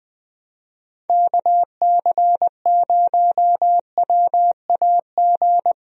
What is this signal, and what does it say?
Morse code, heard as a keyed tone, about 21 wpm, 700 Hz. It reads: KC0WAG
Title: USA Call Signs - 6 Characters - 20wpm